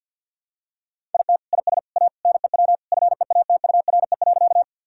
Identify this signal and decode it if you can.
Morse code, heard as a keyed tone, about 52 wpm, 700 Hz. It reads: IT IS A NEW FEATURE.
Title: Sentences from Top 1000 Words - Review Part 5 - 50wpm